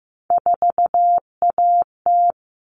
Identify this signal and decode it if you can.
Morse code, heard as a keyed tone, about 15 wpm, 700 Hz. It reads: 4AT
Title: All Cut Numbers - 3-Digit 15wpm